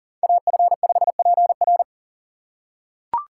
Morse code, heard as a keyed tone, about 40 wpm, 700 Hz. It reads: AF5PR
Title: USA Call Signs - 5 Characters - 4x Character Spacing - 40wpm